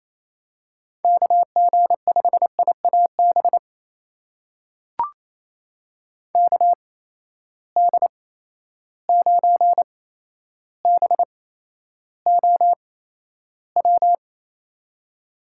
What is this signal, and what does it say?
Morse code, heard as a keyed tone, about 28 wpm, spaced as for 12 wpm, 700 Hz. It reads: KG5IAB KD9BOW
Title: USA Call Signs - 6 Characters - 8x Character Spacing - 28wpm